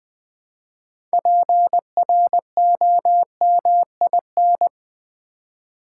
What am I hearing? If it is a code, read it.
Morse code, heard as a keyed tone, about 20 wpm, 700 Hz. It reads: PROMIN
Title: Build A Word - 3 Letters At A Time - 9-Letter Words - 20wpm